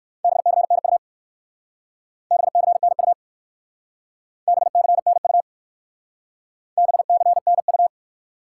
Morse code, heard as a keyed tone, about 61 wpm, 700 Hz. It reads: BCNU BCNU BCNU BCNU